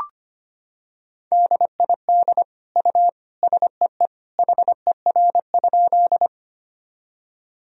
Morse code, heard as a keyed tone, about 25 wpm, 700 Hz. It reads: DID U SEE HER?
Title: Sentences from Top 100 Words and Abbreviations 25wpm